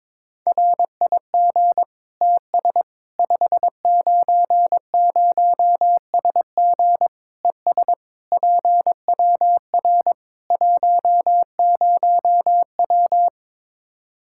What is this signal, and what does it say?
Morse code, heard as a keyed tone, about 22 wpm, 700 Hz. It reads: RIG TS 590SG ES PWR 10W
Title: Practice QSO - Exchange 2 - Info - 22wpm